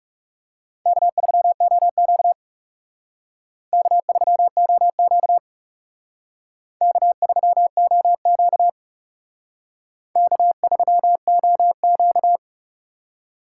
Morse code, heard as a keyed tone, about 37 wpm, 700 Hz. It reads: K3OQ K3OQ K3OQ K3OQ
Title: Speed Racing - USA 1x2 Call Signs 30wpm